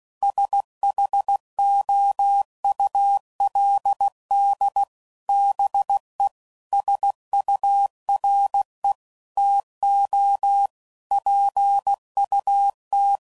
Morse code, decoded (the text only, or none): SHOULD BE SURE TO PUT